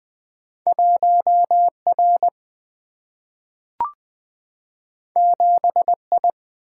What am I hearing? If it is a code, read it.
Morse code, heard as a keyed tone, about 20 wpm, 700 Hz. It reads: 1R 7I
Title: Single-Letter Single-Number 20wpm